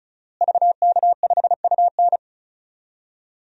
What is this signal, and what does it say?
Morse code, heard as a keyed tone, about 35 wpm, 700 Hz. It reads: VK5UN
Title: DXCC-100 Call Signs - 5 Characters - Repeated 3x - 35wpm